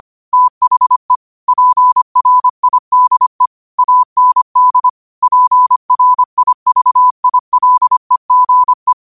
Morse code, heard as a keyed tone, about 25 wpm, 1000 Hz. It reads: THE PRIDE AND PRIVILEGE
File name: qotdfiles / 25wpm